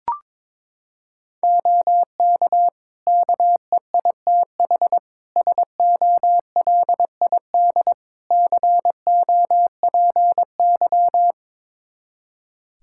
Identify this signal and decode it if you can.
Morse code, heard as a keyed tone, about 22 wpm, 700 Hz. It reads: OK KEITH SOLID COPY
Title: Practice QSO - Exchange 3 - More Info - 22wpm